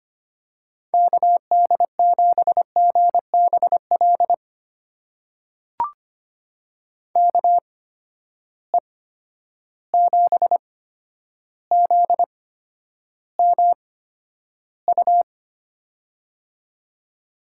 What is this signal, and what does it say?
Morse code, heard as a keyed tone, about 25 wpm, spaced as for 11 wpm, 700 Hz. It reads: KD7GBL KE7ZMU
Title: USA Call Signs - 6 Characters - 8x Character Spacing - 25wpm